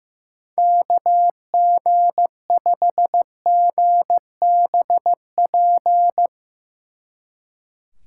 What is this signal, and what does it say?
Morse code, heard as a keyed tone, about 15 wpm, 700 Hz. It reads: KG5GBP
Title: Speed Racing - USA 2x3 Call Signs 15wpm